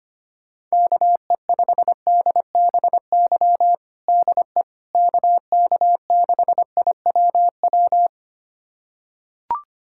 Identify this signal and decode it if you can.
Morse code, heard as a keyed tone, about 25 wpm, 700 Hz. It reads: KE5DBY DE KK6IWW
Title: USA Callsign DE Callsign - 6 Characters - 25wpm